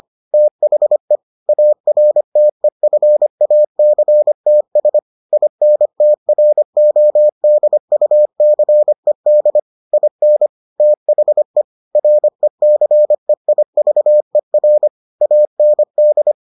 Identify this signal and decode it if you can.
Morse code, heard as a keyed tone, about 25 wpm, 600 Hz. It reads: THE ARTEFACTS INTRODUCED IN THE RECEIVER AND